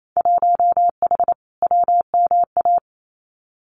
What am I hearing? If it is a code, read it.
Morse code, heard as a keyed tone, about 28 wpm, 700 Hz. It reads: 1H WMA
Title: Winter Field Day Exchanges Encore 28wpm